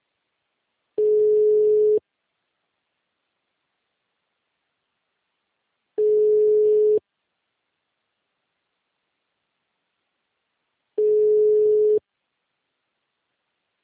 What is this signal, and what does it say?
Ringback tone (Europe, 425 Hz).
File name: out- -2003- - - .32368